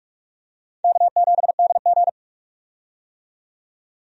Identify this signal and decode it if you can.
Morse code, heard as a keyed tone, about 46 wpm, 700 Hz. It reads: K7DG
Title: USA Call Signs - 4 Characters - 45wpm